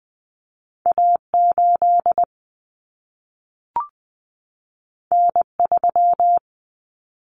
Morse code, heard as a keyed tone, about 20 wpm, 700 Hz. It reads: A8 N3
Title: Common Cut Numbers - 2-Digit 20wpm